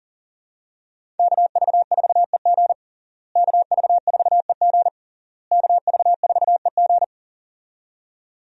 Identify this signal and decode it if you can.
Morse code, heard as a keyed tone, about 40 wpm, 700 Hz. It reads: KV4EG KV4EG KV4EG
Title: USA Call Signs - 5 Characters - Repeated 3x - 40wpm